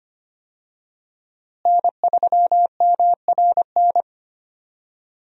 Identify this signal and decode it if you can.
Morse code, heard as a keyed tone, about 25 wpm, 700 Hz. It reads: N3MRN
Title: USA Call Signs - 5 Characters - Repeated 3x - 25wpm